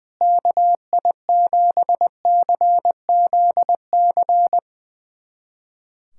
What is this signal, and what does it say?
Morse code, heard as a keyed tone, about 20 wpm, 700 Hz. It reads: KI7CZC
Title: USA Callsign DE Callsign - 6 Characters - 20wpm